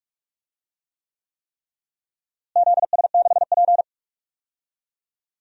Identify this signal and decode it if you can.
Morse code, heard as a keyed tone, about 46 wpm, 700 Hz. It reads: ZS6P
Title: DXCC-100 Call Signs - 4 Characters - Repeated 3x - 45wpm